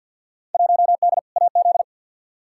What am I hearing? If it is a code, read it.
Morse code, heard as a keyed tone, about 52 wpm, 700 Hz. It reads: 1D AZ
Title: Field Day Exchanges - 50wpm